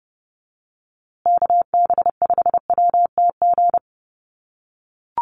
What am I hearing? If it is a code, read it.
Morse code, heard as a keyed tone, about 29 wpm, 700 Hz. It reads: KB5WTG
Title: USA Call Signs Repeated 3x 30wpm